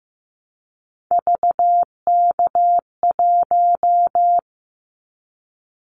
Morse code, heard as a keyed tone, about 15 wpm, 700 Hz. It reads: VK1
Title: Callsign Prefixes for Top 50 DXCC - 15wpm